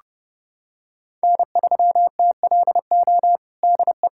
Morse code, heard as a keyed tone, about 30 wpm, 700 Hz. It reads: N3TLO DE
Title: USA Callsign DE Callsign - 5 Characters - 30wpm